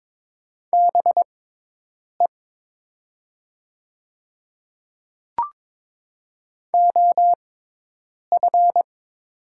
Morse code, heard as a keyed tone, about 22 wpm, spaced as for 8 wpm, 700 Hz. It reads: BE OF
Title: 6x Character Spacing - Top 200 Words - 22wpm